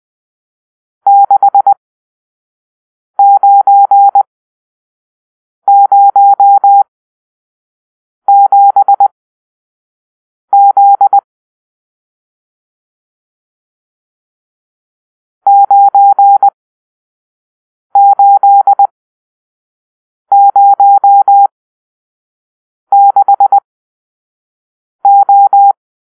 Morse code, heard as a keyed tone, about 20 wpm, spaced as for 5 wpm, 800 Hz. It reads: 6907Z 9806O
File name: arrl / Grp-7 Drill-3